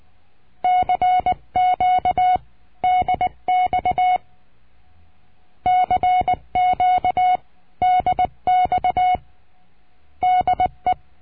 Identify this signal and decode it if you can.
Morse code, heard as a keyed tone, about 19 wpm, 730 Hz. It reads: CQ DX CQ DX DE
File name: cursos / CURSO DE TELEGRAFIA / Telegrafia - Trilha 22